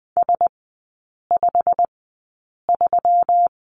Morse code, heard as a keyed tone, about 20 wpm, spaced as for 8 wpm, 700 Hz. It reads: S53
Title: 121 Lesson 20e - Number 3 - Three Characters 20wpm